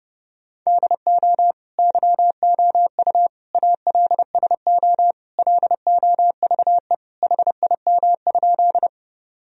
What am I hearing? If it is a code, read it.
Morse code, heard as a keyed tone, about 32 wpm, 700 Hz. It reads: DO YOU ALSO LOVE HIM?
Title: Sentences from Top 1000 Words - Review Part 1 - 30wpm